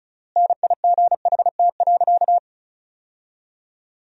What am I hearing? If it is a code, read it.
Morse code, heard as a keyed tone, about 34 wpm, 700 Hz. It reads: NIGHT.
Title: 3x Word-Spacing - Sentences from Top 500 Words - 35wpm